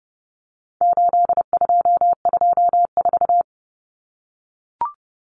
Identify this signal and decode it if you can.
Morse code, heard as a keyed tone, about 30 wpm, 700 Hz. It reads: 8224
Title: Common Cut Numbers - 4-Digit 30wpm